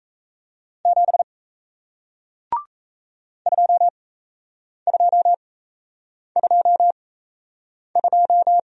Morse code, heard as a keyed tone, about 38 wpm, spaced as for 8 wpm, 700 Hz. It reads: 7 2222